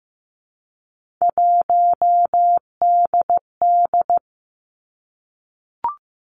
Morse code, heard as a keyed tone, about 15 wpm, 700 Hz. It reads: 1DD
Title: Three Challenging Letter-Number Combinations 15wpm